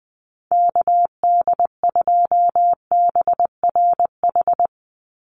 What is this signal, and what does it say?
Morse code, heard as a keyed tone, about 20 wpm, 700 Hz. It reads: KD2BRH
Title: USA Call Signs 20wpm